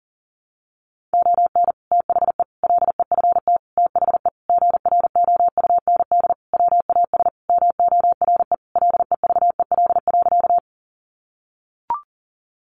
Morse code, heard as a keyed tone, about 40 wpm, 700 Hz. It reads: ON THE LEFT THE GROUND WAS MORE LEVEL.
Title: Sentences from Top 700 Words 40wpm